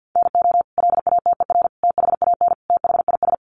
Morse code, heard as a keyed tone, about 50 wpm, 700 Hz. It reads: NO LATER THAN THIS
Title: Sentences from Top 700 Words 50wpm